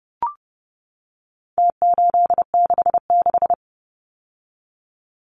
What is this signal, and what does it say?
Morse code, heard as a keyed tone, about 30 wpm, 700 Hz. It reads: T866
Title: Numbers - Contest Style - 4-Digit 30wpm